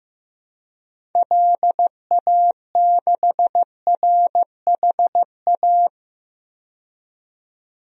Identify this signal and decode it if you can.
Morse code, heard as a keyed tone, about 15 wpm, 700 Hz. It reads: LA6RHA
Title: DXCC-100 Call Signs - 6 Characters - 15wpm